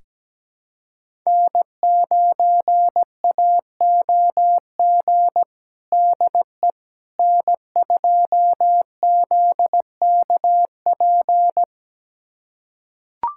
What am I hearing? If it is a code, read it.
Morse code, heard as a keyed tone, about 17 wpm, 700 Hz. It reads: N9AOG DE N2ZKP
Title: USA Callsign DE Callsign - 5 Characters - 17wpm